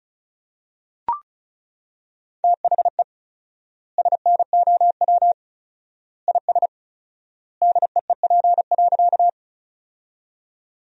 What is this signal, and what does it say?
Morse code, heard as a keyed tone, about 34 wpm, 700 Hz. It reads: THE SNOW IS DEEP.
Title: 4x Word-Spacing - Sentences from Top 500 Words - 35wpm